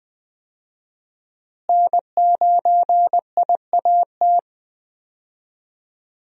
Morse code, heard as a keyed tone, about 20 wpm, 700 Hz. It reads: N9IAT
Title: USA Call Signs - 5 Characters - Repeated 2x - 20wpm